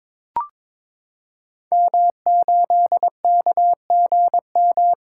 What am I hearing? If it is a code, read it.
Morse code, heard as a keyed tone, about 22 wpm, 700 Hz. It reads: M8KGM
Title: Code Groups - 5 Letter:Numbers - 22wpm